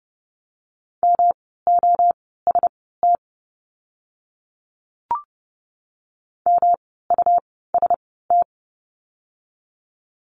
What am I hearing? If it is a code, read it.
Morse code, heard as a keyed tone, about 29 wpm, spaced as for 17 wpm, 700 Hz. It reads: MOST MUST
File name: course / 30:10wpm / 237 Lesson 38f - Letter q - Words 30wpm